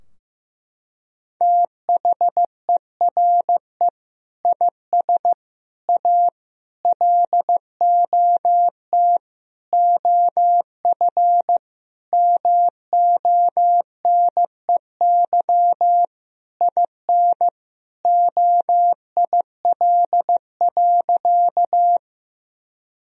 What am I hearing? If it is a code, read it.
Morse code, heard as a keyed tone, about 15 wpm, 700 Hz. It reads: THERE IS A LOT OF MONEY IN OIL.
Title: Sentences from Top 1000 Words - Review Part 4 - 15wpm